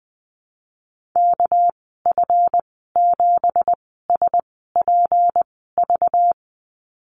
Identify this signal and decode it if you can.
Morse code, heard as a keyed tone, about 20 wpm, spaced as for 15 wpm, 700 Hz. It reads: KF7SPV
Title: 207 Lesson 33g - Letter j - Call Signs 20wpm